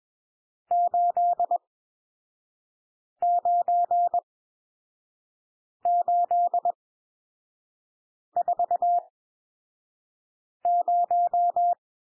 Morse code, heard as a keyed tone, about 21 wpm, spaced as for 5 wpm, 700 Hz. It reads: 89840
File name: cwmp3 / N1 SOUND NUMBERS 21 wpm